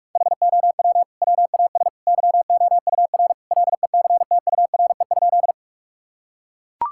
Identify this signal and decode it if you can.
Morse code, heard as a keyed tone, about 41 wpm, 700 Hz. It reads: HOW WAS YOUR LECTURE?
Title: Sentences from Top 1600 Words 45wpm